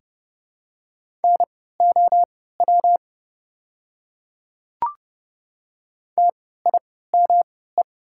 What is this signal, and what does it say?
Morse code, heard as a keyed tone, about 30 wpm, spaced as for 17 wpm, 700 Hz. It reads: NOW TIME
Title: 3x Character Spacing - Top 100 Words - 30wpm